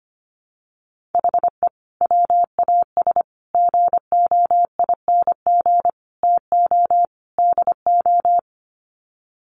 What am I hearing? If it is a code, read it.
Morse code, heard as a keyed tone, about 25 wpm, 700 Hz. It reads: HE WAS GOING TO DO
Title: Sets of 5 Words Encore2 25wpm